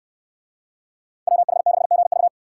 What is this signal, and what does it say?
Morse code, heard as a keyed tone, about 63 wpm, 700 Hz. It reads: WH6GV